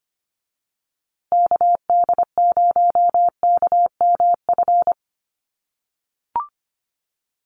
Morse code, heard as a keyed tone, about 25 wpm, 700 Hz. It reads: KD0KMF
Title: USA Call Signs Repeated 2x 25wpm